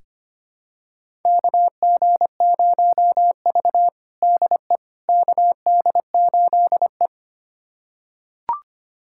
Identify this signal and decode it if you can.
Morse code, heard as a keyed tone, about 25 wpm, 700 Hz. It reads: KG0V DE KD8E
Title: USA Callsign DE Callsign - 4 Characters - 25wpm